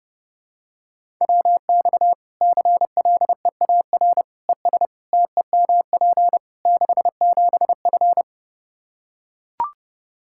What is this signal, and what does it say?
Morse code, heard as a keyed tone, about 32 wpm, 700 Hz. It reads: WX CLEAR ES TEMP 67F
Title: K1USN SST QSO Practice 30wpm